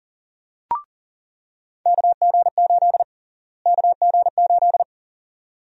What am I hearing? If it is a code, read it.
Morse code, heard as a keyed tone, about 40 wpm, 700 Hz. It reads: KG8 KG8
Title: USA Call Sign Prefixes - Repeated 2x - 40wpm